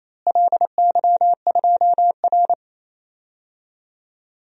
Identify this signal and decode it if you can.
Morse code, heard as a keyed tone, about 28 wpm, 700 Hz. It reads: LY2R
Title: DXCC-100 Call Signs - 4 Characters - Repeated 2x - 28wpm